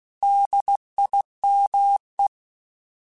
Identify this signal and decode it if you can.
Morse code, heard as a keyed tone, about 16 wpm, 800 Hz. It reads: DIME
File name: Alphabetical / Alphabetical mp3s / Ds @15